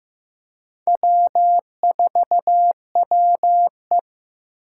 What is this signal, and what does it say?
Morse code, heard as a keyed tone, about 15 wpm, 700 Hz. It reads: W4WE
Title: Four Challenging Letter-Number Combinations 15wpm